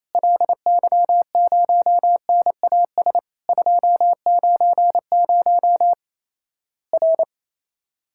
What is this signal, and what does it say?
Morse code, heard as a keyed tone, about 27 wpm, 700 Hz. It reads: LY0NAS 290 R
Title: Worked All Europe DX Contest - QTC Exchange - 28wpm